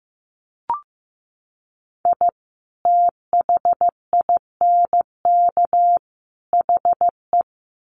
Morse code, heard as a keyed tone, about 15 wpm, 700 Hz. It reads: I THINK HE
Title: Sets of 3 Words 15wpm